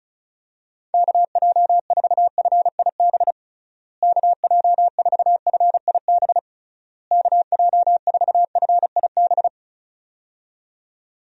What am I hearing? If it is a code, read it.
Morse code, heard as a keyed tone, about 36 wpm, 700 Hz. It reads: KJ4FIB KJ4FIB KJ4FIB
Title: USA Call Signs - 6 Characters - Repeated 3x - 35wpm